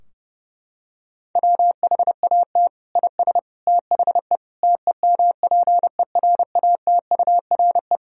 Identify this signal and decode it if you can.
Morse code, heard as a keyed tone, about 30 wpm, 700 Hz. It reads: WHAT IS THE TEMPERATURE
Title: Sentences from Top 1000 Words - Review Part 4 - 30wpm